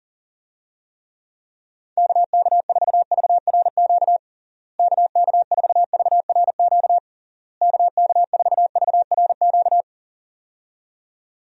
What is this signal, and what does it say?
Morse code, heard as a keyed tone, about 40 wpm, 700 Hz. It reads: KK4VRQ KK4VRQ KK4VRQ
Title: USA Call Signs - 6 Characters - Repeated 3x - 40wpm